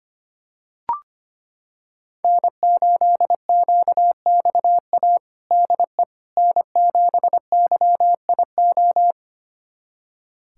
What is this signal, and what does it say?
Morse code, heard as a keyed tone, about 25 wpm, 700 Hz. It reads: N8QXA DE N7YIO
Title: USA Callsign DE Callsign - 5 Characters - 25wpm